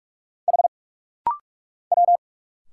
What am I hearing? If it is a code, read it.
Morse code, heard as a keyed tone, about 46 wpm, 700 Hz. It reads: H W
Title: Single Letters - Rapid-Fire 45wpm